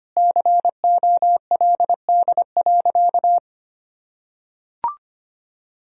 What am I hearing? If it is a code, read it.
Morse code, heard as a keyed tone, about 25 wpm, 700 Hz. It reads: COLD.
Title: Sentences from Top 1000 Words - Review Part 1 - 25wpm